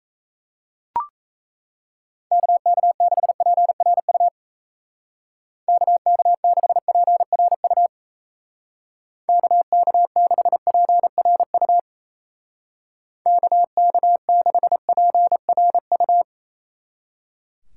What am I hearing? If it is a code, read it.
Morse code, heard as a keyed tone, about 34 wpm, 700 Hz. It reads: KK6PRU KK6PRU KK6PRU KK6PRU
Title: Speed Racing - USA 2x3 Call Signs 28wpm